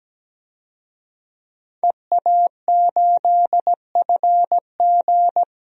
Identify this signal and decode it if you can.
Morse code, heard as a keyed tone, about 17 wpm, 700 Hz. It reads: EA8FG
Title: DXCC-100 Call Signs - 5 Characters - 17wpm